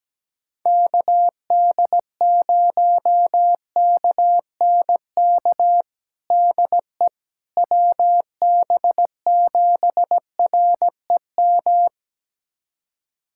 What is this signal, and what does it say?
Morse code, heard as a keyed tone, about 17 wpm, 700 Hz. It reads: KD0KNK DE WB7REM
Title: USA Callsign DE Callsign - 6 Characters - 17wpm